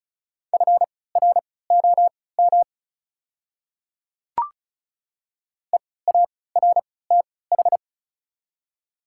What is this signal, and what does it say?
Morse code, heard as a keyed tone, about 34 wpm, spaced as for 20 wpm, 700 Hz. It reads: FROM EARTH
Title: 3x Character Spacing - Top 300 Words - 35wpm